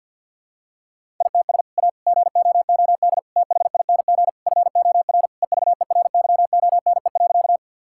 Morse code, heard as a keyed tone, about 51 wpm, 700 Hz. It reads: ITS A GOOD THING FOR EVERYONE.
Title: Sentences from Top 1000 Words - Review Part 5 - 50wpm